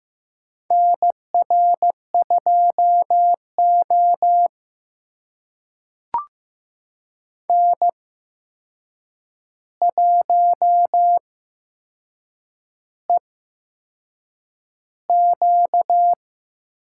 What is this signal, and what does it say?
Morse code, heard as a keyed tone, about 15 wpm, spaced as for 6 wpm, 700 Hz. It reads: NR2O N1EQ
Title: USA Call Signs - 4 Characters - 8x Character Spacing - 15wpm